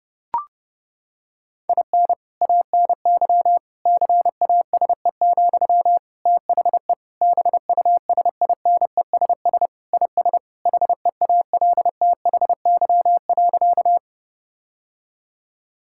Morse code, heard as a keyed tone, about 30 wpm, 700 Hz. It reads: IN ANY CASE, THE BUSINESS IS HEALTHY.
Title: Sentences from Top 1000 Words - Review Part 5 - 30wpm